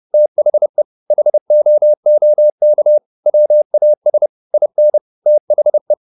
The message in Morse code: THE HOOK WAS IN THE

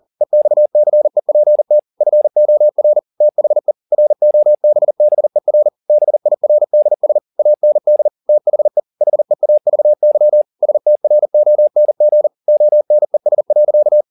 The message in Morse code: EXCEPT FOR THE ROBBER BIRDS AND THE HEAVY STRONG ONES.

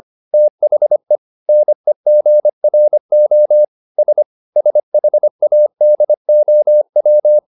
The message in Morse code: THE NEGRO S SHADOW

